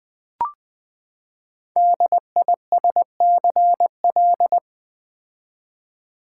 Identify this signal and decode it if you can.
Morse code, heard as a keyed tone, about 20 wpm, 700 Hz. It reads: DISCL
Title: Build A Word - 5 Letters At A Time - 10-Letter Words - 20wpm